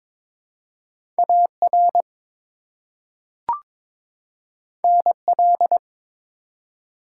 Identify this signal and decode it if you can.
Morse code, heard as a keyed tone, about 22 wpm, 700 Hz. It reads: AR NL
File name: practice / 22wpm / ARRL Sections v2 - 22wpm